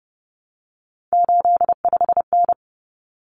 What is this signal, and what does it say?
Morse code, heard as a keyed tone, about 30 wpm, 700 Hz. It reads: 85N
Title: Common Cut Numbers - 3-Digit 30wpm